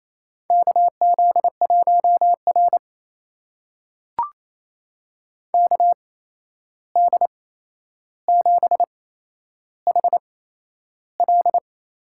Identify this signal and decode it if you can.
Morse code, heard as a keyed tone, about 27 wpm, spaced as for 11 wpm, 700 Hz. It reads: KZ1R KD7HL
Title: USA Call Signs - 8x Character Spacing - 28wpm